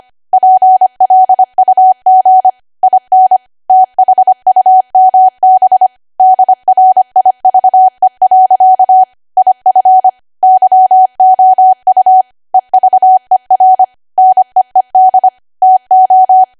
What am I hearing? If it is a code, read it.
Morse code, heard as a keyed tone, about 25 wpm, 750 Hz. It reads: PLUG IN THUMB DRIVE. IF YOU EVER NEED TO
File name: cwa / W1AW-Level3 / 25WPM